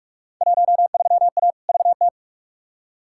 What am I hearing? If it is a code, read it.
Morse code, heard as a keyed tone, about 46 wpm, 700 Hz. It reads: 13A VT
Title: Field Day Exchanges Encore - 45wpm